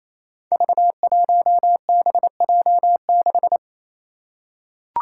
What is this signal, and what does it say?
Morse code, heard as a keyed tone, about 28 wpm, 700 Hz. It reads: V1BJ6
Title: Five Challenging Letter-Number Combinations 28wpm